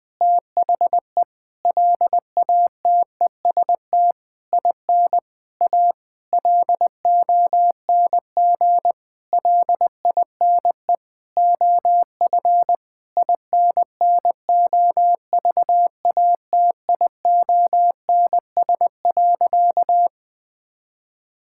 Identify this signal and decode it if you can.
Morse code, heard as a keyed tone, about 21 wpm, 700 Hz. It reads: THE LATEST IN A LONG LINE OF INNOVATIONS.
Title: Sentences with 8-Word Phrases - 20wpm